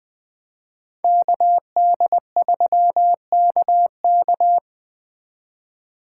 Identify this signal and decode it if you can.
Morse code, heard as a keyed tone, about 24 wpm, 700 Hz. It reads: KD3KK
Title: Speed Racing - USA 2x2 Call Signs 20wpm